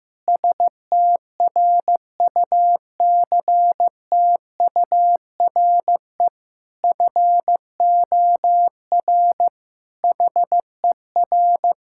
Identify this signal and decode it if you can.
Morse code, heard as a keyed tone, about 15 wpm, 700 Hz. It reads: STRUCTURE FOR HER
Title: Sentences from Top 1000 Words - Review Part 6 - 15wpm